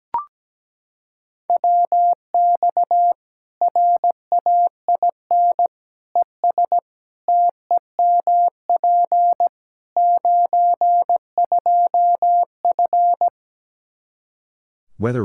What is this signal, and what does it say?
Morse code, heard as a keyed tone, about 17 wpm, 700 Hz. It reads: WX RAIN ES TEMP 92F
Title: Practice QSO - Exchange 2 - Info - 17wpm